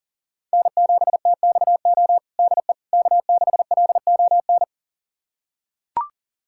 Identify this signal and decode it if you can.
Morse code, heard as a keyed tone, about 40 wpm, 700 Hz. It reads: N7TXO DE K6LON
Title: USA Callsign DE Callsign - 5 Characters - 40wpm